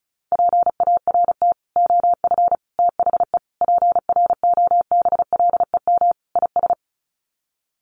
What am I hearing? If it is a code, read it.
Morse code, heard as a keyed tone, about 34 wpm, 700 Hz. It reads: PART OF THE PROBLEM IS
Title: Sets of 5 Words Encore2 35wpm